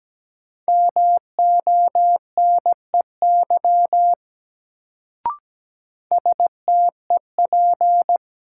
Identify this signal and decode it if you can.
Morse code, heard as a keyed tone, about 17 wpm, 700 Hz. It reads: MONEY STEP
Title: Sentences from Top 400 Words - New Word Drill - 17wpm